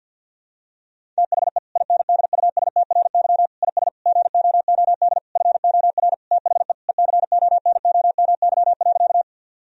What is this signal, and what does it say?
Morse code, heard as a keyed tone, about 50 wpm, 700 Hz. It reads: THE INDUSTRY IS GOOD FOR THE ECONOMY.
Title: Sentences from Top 1000 Words - Review Part 4 - 50wpm